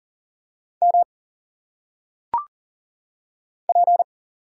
Morse code, heard as a keyed tone, about 39 wpm, 700 Hz. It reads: M P
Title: Single Letters - Phonetic Pronunciation 40wpm